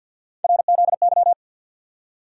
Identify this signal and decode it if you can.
Morse code, heard as a keyed tone, about 52 wpm, 690 Hz. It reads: RZY
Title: USA Call Sign Postfixes - 3 Letters 50wpm